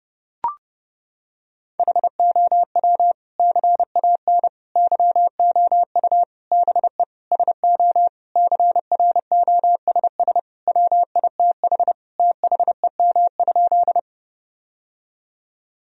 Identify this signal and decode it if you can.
Morse code, heard as a keyed tone, about 32 wpm, 700 Hz. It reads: HOW CAN YOU BE SO CROSS WITH THEM?
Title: Sentences from Top 1000 Words - Review Part 1 - 30wpm